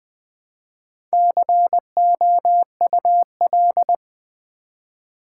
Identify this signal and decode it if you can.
Morse code, heard as a keyed tone, about 20 wpm, 700 Hz. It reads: COUL
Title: Build A Word - 1 Letter At A Time - 5-Letter Words - 20wpm